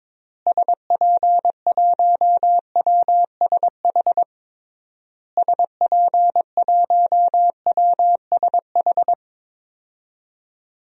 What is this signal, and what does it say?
Morse code, heard as a keyed tone, about 22 wpm, 700 Hz. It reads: SP1WSH SP1WSH
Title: DXCC-100 Call Signs - 6 Characters - Repeated 3x - 22wpm